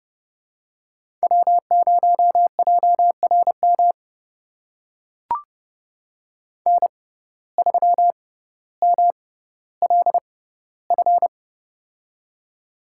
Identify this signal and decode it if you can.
Morse code, heard as a keyed tone, about 31 wpm, spaced as for 16 wpm, 700 Hz. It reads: W0JRM N3MLF
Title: USA Call Signs - 5 Characters - 6x Character Spacing - 30wpm